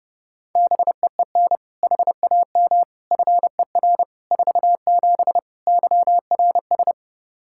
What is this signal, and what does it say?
Morse code, heard as a keyed tone, about 30 wpm, 700 Hz. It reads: BEEN HAM FER 47 YRS
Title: Practice QSO - Exchange 3 - More Info - 30wpm